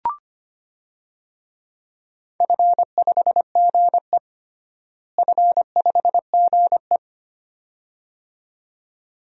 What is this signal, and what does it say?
Morse code, heard as a keyed tone, about 25 wpm, 700 Hz. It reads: F5GE F5GE
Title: DXCC-100 Call Signs - 4 Characters - Repeated 2x - 25wpm